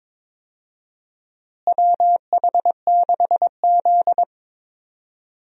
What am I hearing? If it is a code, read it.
Morse code, heard as a keyed tone, about 22 wpm, 700 Hz. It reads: WH6Z
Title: USA Call Signs - 4 Characters - Repeated 2x - 22wpm